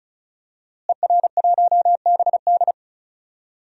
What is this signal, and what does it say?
Morse code, heard as a keyed tone, about 36 wpm, 700 Hz. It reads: ER1BD
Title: DXCC-100 Call Signs - 5 Characters - Repeated 3x - 35wpm